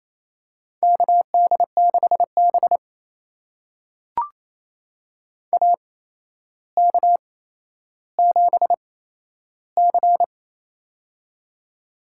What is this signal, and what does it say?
Morse code, heard as a keyed tone, about 28 wpm, spaced as for 12 wpm, 700 Hz. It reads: KD6B AK7C
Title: USA Call Signs - 4 Characters - 8x Character Spacing - 28wpm